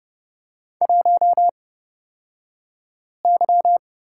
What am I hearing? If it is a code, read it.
Morse code, heard as a keyed tone, about 30 wpm, spaced as for 5 wpm, 700 Hz. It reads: 1Y
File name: Lesson 38d - Letter q - Two Characters 30wpm